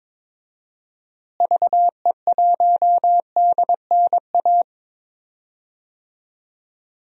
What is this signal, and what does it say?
Morse code, heard as a keyed tone, about 21 wpm, 700 Hz. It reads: VE1DNA
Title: DXCC-100 Call Signs - 6 Characters - Repeated 3x - 22wpm